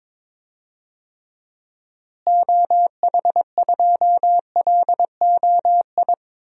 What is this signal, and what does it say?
Morse code, heard as a keyed tone, about 23 wpm, 700 Hz. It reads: OH2LOI
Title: DXCC-100 Call Signs - 6 Characters - Repeated 3x - 22wpm